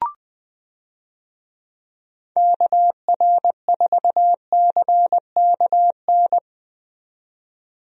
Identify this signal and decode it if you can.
Morse code, heard as a keyed tone, about 21 wpm, 700 Hz. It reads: KR4CKN
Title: USA Call Signs - 6 Characters - 20wpm